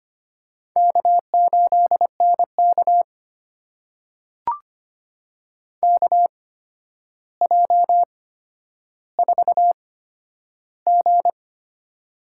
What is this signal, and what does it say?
Morse code, heard as a keyed tone, about 25 wpm, spaced as for 11 wpm, 700 Hz. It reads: K8NK KJ4G
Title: USA Call Signs - 4 Characters - 8x Character Spacing - 25wpm